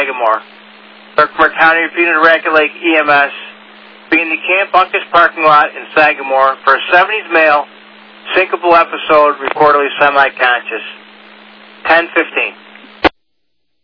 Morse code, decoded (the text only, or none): none